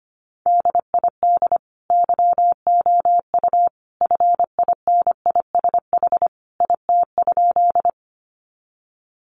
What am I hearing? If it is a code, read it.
Morse code, heard as a keyed tone, about 25 wpm, 700 Hz. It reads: DID YOU FINISH IT?
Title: Sentences from Top 1200 Words 25wpm